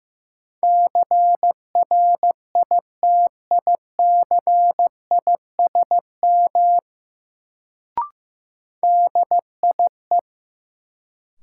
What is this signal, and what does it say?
Morse code, heard as a keyed tone, about 15 wpm, 700 Hz. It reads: CRITICISM DIE
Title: Sentences from Top 1300 Words - New Word Drill - 15wpm